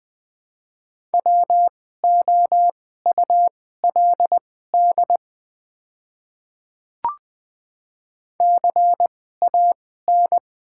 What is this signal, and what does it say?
Morse code, heard as a keyed tone, about 20 wpm, spaced as for 14 wpm, 700 Hz. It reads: WOULD CAN
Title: 2x Character Spacing - Top 100 Words - 20wpm